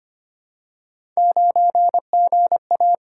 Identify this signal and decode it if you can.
Morse code, heard as a keyed tone, about 25 wpm, 700 Hz. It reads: 9GA
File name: Three Letter-Number Combinations - 25wpm